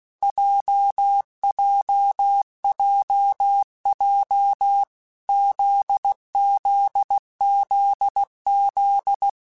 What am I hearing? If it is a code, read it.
Morse code, heard as a keyed tone, about 16 wpm, 800 Hz. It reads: JJJJ ZZZZ
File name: Basic Letter Groups By7Groups / JZ17 / JZ17@15